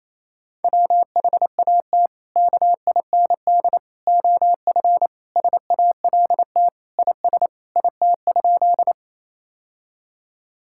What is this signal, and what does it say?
Morse code, heard as a keyed tone, about 27 wpm, 700 Hz. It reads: WHAT KIND OF SALT IS IT?
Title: Sentences from Top 1000 Words - Review Part 5 - 28wpm